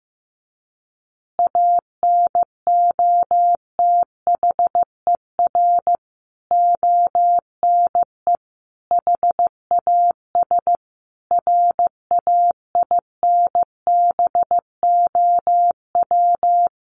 Morse code, heard as a keyed tone, about 15 wpm, 700 Hz. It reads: ANOTHER ONE HAS RAINBOW
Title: Additional Easy Phrases 15wpm